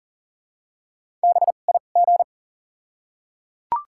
Morse code, heard as a keyed tone, about 37 wpm, spaced as for 28 wpm, 700 Hz. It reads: BIG
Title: 2x Character Spacing - Top 200 Words - 40wpm